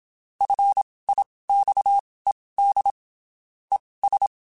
Morse code, heard as a keyed tone, about 27 wpm, spaced as for 19 wpm, 800 Hz. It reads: FIXED ES
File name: (spaced) / QSOs Spaced-mp3s / WA7QOX 1 @25 sp